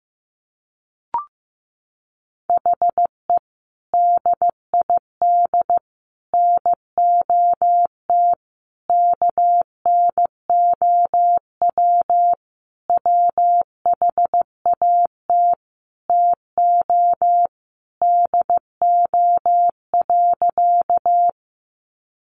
Morse code, heard as a keyed tone, about 15 wpm, 700 Hz. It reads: HE DID NOT KNOW WHAT TO DO.
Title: Sentences from Top 600 Words 15wpm